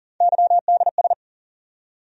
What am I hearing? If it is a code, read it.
Morse code, heard as a keyed tone, about 37 wpm, 700 Hz. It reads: YDS
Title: USA Call Sign Postfixes - 3 Letters 40wpm